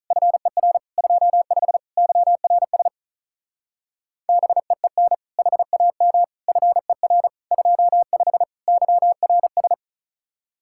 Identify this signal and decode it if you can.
Morse code, heard as a keyed tone, about 43 wpm, 700 Hz. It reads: FER 25 YRS BEEN HAM FER 25 YRS
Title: Speed-Racing Common QSO Phrases - 35wpm